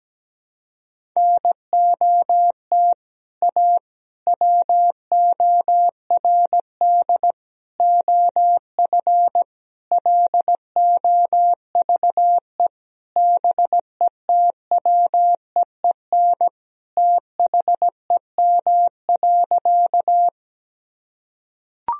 Morse code, heard as a keyed tone, about 17 wpm, 700 Hz. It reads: NOT A WORD OF LOVE BETWEEN THEM.
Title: Sentences from Top 1000 Words - Review Part 1 - 17wpm